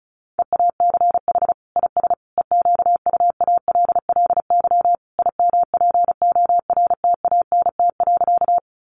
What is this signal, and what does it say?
Morse code, heard as a keyed tone, about 34 wpm, 700 Hz. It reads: EACH IS EQUALLY IMPORTANT.
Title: Sentences from Top 1400 Words 35wpm